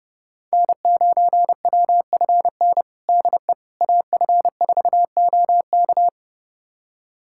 Morse code, heard as a keyed tone, about 31 wpm, 700 Hz. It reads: N9WFN DE AF4OK
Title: USA Callsign DE Callsign - 5 Characters - 30wpm